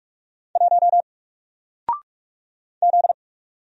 Morse code, heard as a keyed tone, about 45 wpm, 700 Hz. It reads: 1 Z
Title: Single Letter-Number 45wpm